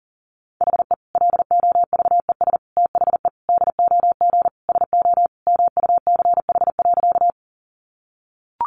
Morse code, heard as a keyed tone, about 40 wpm, 700 Hz. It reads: HE LOVES THE DOG SO MUCH.
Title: Additional Easy Phrases 40wpm